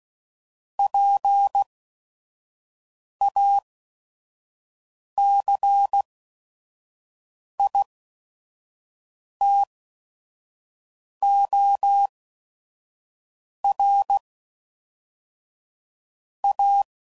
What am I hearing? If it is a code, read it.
Morse code, heard as a keyed tone, about 16 wpm, spaced as for 5 wpm, 800 Hz. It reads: PACITOR A
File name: CW Code Groups @5WPM / Technical Subjects / XC